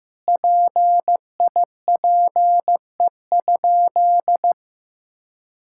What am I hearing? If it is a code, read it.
Morse code, heard as a keyed tone, about 15 wpm, 700 Hz. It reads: PIPE?
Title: Sentences from Top 1600 Words 15wpm